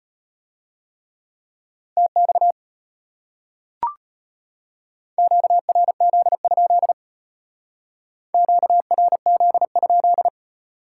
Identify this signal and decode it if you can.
Morse code, heard as a keyed tone, about 33 wpm, 700 Hz. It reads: TX QRZ? QRZ?